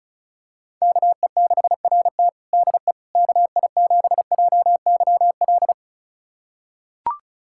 Morse code, heard as a keyed tone, about 34 wpm, 700 Hz. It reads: KE6RT DE KI7JYL
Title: USA Callsign DE Callsign 35wpm